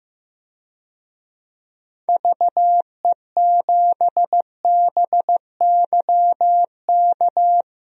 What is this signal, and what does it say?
Morse code, heard as a keyed tone, about 16 wpm, 700 Hz. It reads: VE7BYK
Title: DXCC-100 Call Signs - 6 Characters - Repeated 2x - 15wpm